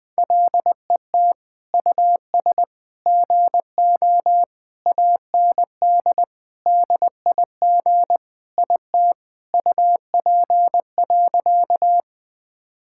Morse code, heard as a keyed tone, about 21 wpm, 700 Hz. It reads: LET US GO AND DIG IT UP.
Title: Sentences from Top 1600 Words 20wpm